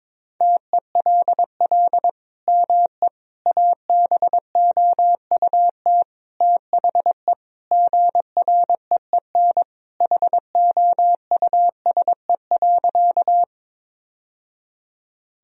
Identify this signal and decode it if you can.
Morse code, heard as a keyed tone, about 22 wpm, 700 Hz. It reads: TELL ME ABOUT THE GREEN HOUSE.
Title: Sentences from Top 1000 Words - Review Part 2 - 22wpm